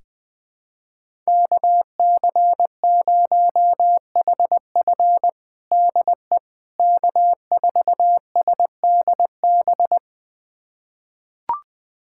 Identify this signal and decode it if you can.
Morse code, heard as a keyed tone, about 20 wpm, 700 Hz. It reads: KC0HF DE K4SDB
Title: USA Callsign DE Callsign - 5 Characters - 20wpm